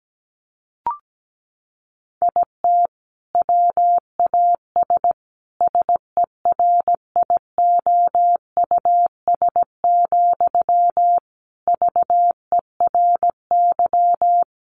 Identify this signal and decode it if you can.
Morse code, heard as a keyed tone, about 17 wpm, 700 Hz. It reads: IT WAS SERIOUS, VERY
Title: Sentences from Top 1100 Words 17wpm